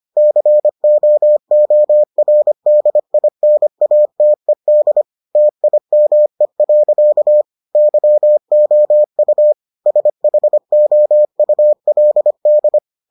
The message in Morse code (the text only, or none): COORDINATED TIME. YOU SHOULD